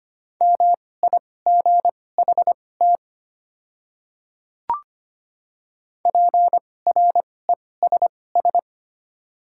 Morse code, heard as a keyed tone, about 25 wpm, spaced as for 18 wpm, 690 Hz. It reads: MIGHT PRESS
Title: 2x Character Spacing - Top 300 Words - 25wpm